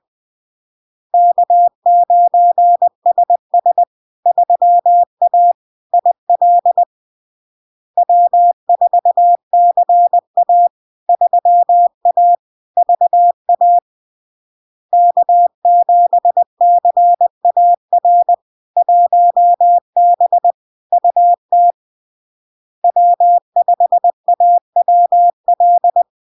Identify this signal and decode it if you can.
Morse code, heard as a keyed tone, about 20 wpm, 700 Hz. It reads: K9SS 3A IL W4CA 3A VA K7CAR 1B UT W5AWL
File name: LARK-FD-2022-Log-in-CW-20wpm-700-Hz